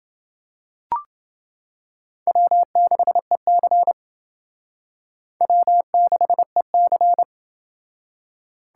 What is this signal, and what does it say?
Morse code, heard as a keyed tone, about 28 wpm, 700 Hz. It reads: W6EC W6EC